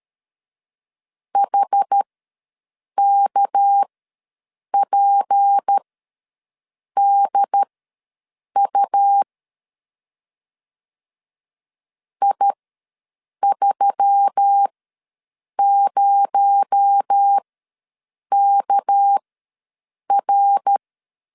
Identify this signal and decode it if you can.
Morse code, heard as a keyed tone, about 13 wpm, spaced as for 7 wpm, 800 Hz. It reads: HKPDU I30KR
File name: LEZIONI / 6 PKX0 / 6 PKX0 ws 12wpm 3448